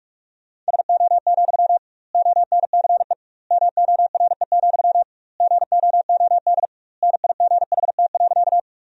Morse code, heard as a keyed tone, about 41 wpm, 700 Hz. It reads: SO, ONCE MORE, GOOD NIGHT.
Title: 2x Word-Spacing - Sentences from Top 300 Words - 45wpm